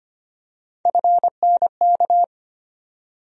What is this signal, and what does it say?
Morse code, heard as a keyed tone, about 25 wpm, 700 Hz. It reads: FNK
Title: USA Call Sign Postfixes - 3 Letters 25wpm